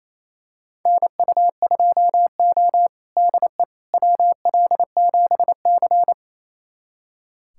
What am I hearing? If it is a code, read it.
Morse code, heard as a keyed tone, about 28 wpm, 700 Hz. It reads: NU2O DE WL7C
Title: USA Callsign DE Callsign - 4 Characters - 28wpm